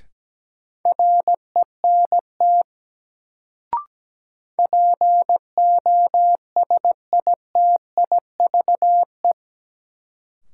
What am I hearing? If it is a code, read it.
Morse code, heard as a keyed tone, about 17 wpm, 700 Hz. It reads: RENT POSITIVE
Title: Sentences from Top 1000 Words - Review Part 5 - New Word Drill - 17wpm